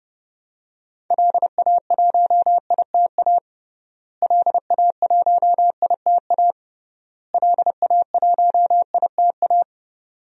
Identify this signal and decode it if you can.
Morse code, heard as a keyed tone, about 28 wpm, 700 Hz. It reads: LA1ITA LA1ITA LA1ITA
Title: DXCC-100 Call Signs - 6 Characters - Repeated 3x - 30wpm